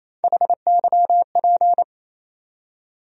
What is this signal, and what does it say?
Morse code, heard as a keyed tone, about 28 wpm, 700 Hz. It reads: HYP
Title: USA Call Sign Postfixes - 3 Letters 28wpm